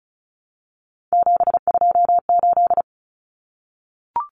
Morse code, heard as a keyed tone, about 34 wpm, 700 Hz. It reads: 728
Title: Common Cut Numbers - 3-Digit 35wpm